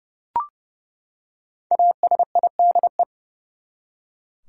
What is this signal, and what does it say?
Morse code, heard as a keyed tone, about 30 wpm, 700 Hz. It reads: ASIDE
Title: Five Letter Words - 30wpm